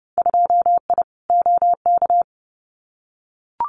Morse code, heard as a keyed tone, about 30 wpm, 700 Hz. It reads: 2I OK
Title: Winter Field Day Exchanges Encore 30wpm